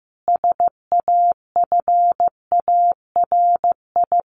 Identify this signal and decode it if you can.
Morse code, heard as a keyed tone, about 15 wpm, 700 Hz. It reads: SAFARI
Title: A Sound Of Thunder 15wpm